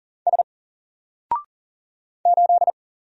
Morse code, heard as a keyed tone, about 40 wpm, 700 Hz. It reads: S 8
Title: Single Character (Letter:Number:Punctuation:Prosign) Alt 40wpm